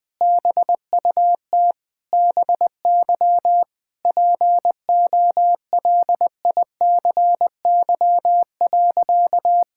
Morse code, heard as a keyed tone, about 21 wpm, 700 Hz. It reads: BUT BY POLICY.
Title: Sentences from Top 1000 Words - Review Part 5 - 20wpm